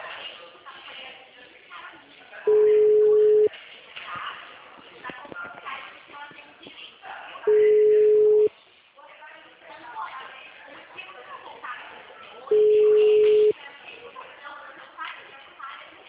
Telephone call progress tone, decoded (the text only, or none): ringback tone